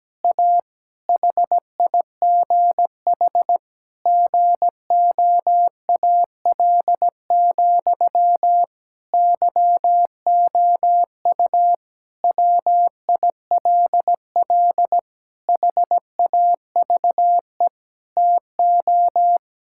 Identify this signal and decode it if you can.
Morse code, heard as a keyed tone, about 17 wpm, 700 Hz. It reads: A HIGH GOAL, YOU WILL HAVE TO
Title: Sentences from Top 1000 Words - Review Part 6 - 17wpm